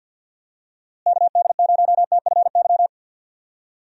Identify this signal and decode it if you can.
Morse code, heard as a keyed tone, about 50 wpm, 700 Hz. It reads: KD0TFY
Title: USA Call Signs - 6 Characters - Repeated 2x - 50wpm